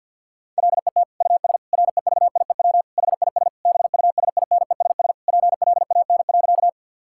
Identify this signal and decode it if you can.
Morse code, heard as a keyed tone, about 50 wpm, 700 Hz. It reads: LET US REVIEW HIS BUSINESS PLAN.
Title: Sentences from Top 1000 Words - Review Part 6 - 50wpm